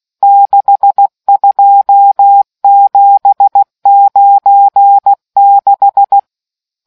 Morse code, heard as a keyed tone, about 16 wpm, 800 Hz. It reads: 62796